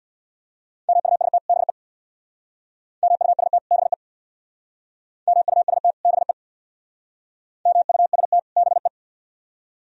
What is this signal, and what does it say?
Morse code, heard as a keyed tone, about 63 wpm, 700 Hz. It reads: MUST BE MUST BE MUST BE MUST BE